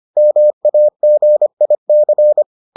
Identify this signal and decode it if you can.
Morse code, heard as a keyed tone, about 25 wpm, 600 Hz. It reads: MAGIC